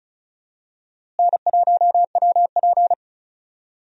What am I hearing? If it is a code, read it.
Morse code, heard as a keyed tone, about 34 wpm, 700 Hz. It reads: N1WP
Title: USA Call Signs - 4 Characters - Repeated 3x - 35wpm